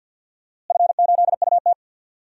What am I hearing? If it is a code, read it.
Morse code, heard as a keyed tone, about 50 wpm, 700 Hz. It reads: F8UT
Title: DXCC-100 Call Signs - 4 Characters - Repeated 3x - 50wpm